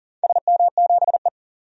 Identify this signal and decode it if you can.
Morse code, heard as a keyed tone, about 40 wpm, 700 Hz. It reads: SM7E
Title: DXCC-100 Call Signs - 4 Characters - 40wpm